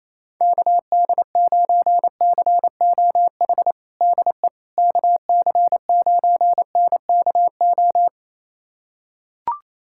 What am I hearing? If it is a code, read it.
Morse code, heard as a keyed tone, about 28 wpm, 700 Hz. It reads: KD9COH DE KC9NKO
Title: USA Callsign DE Callsign - 6 Characters - 28wpm